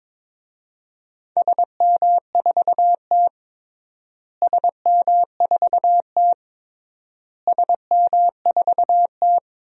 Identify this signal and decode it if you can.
Morse code, heard as a keyed tone, about 22 wpm, 700 Hz. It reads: SM4T SM4T SM4T
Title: DXCC-100 Call Signs - 4 Characters - Repeated 3x - 22wpm